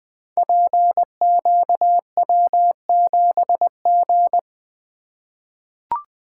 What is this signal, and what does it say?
Morse code, heard as a keyed tone, about 20 wpm, 700 Hz. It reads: PQW7G
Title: Code Groups - 5 Letter:Numbers - 20wpm